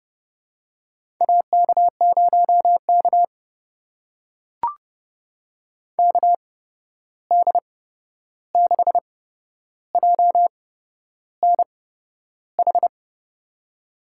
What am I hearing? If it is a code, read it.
Morse code, heard as a keyed tone, about 32 wpm, spaced as for 11 wpm, 700 Hz. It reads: AK0K KD6JNH
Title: USA Call Signs - 8x Character Spacing - 30wpm